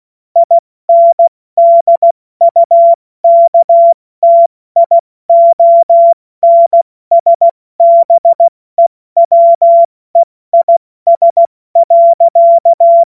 Morse code, heard as a keyed tone, about 16 wpm, 680 Hz. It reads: INDUKTIONSBEWEIS.
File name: Morsekurs / Witze / Witz 12. April 2022 70 F80